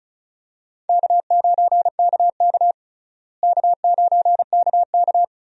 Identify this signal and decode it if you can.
Morse code, heard as a keyed tone, about 34 wpm, 700 Hz. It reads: K9KK K9KK
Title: USA Call Signs - 4 Characters - Repeated 3x - 35wpm